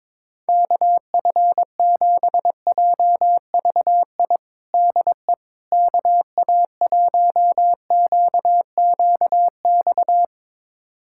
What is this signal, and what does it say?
Morse code, heard as a keyed tone, about 22 wpm, 700 Hz. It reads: KF7JVI DE KA1QQX
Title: USA Callsign DE Callsign - 6 Characters - 22wpm